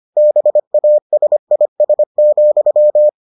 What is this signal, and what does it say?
Morse code, heard as a keyed tone, about 25 wpm, 600 Hz. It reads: BASIS,